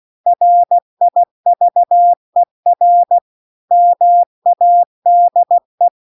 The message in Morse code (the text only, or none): RIVER MADE